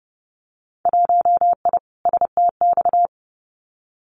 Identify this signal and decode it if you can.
Morse code, heard as a keyed tone, about 30 wpm, 700 Hz. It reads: 1I STX
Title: Winter Field Day Exchanges 30wpm